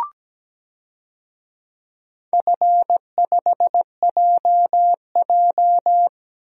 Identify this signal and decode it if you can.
Morse code, heard as a keyed tone, about 17 wpm, 700 Hz. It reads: F5JJ
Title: DXCC-100 Call Signs - 4 Characters - Repeated 3x - 17wpm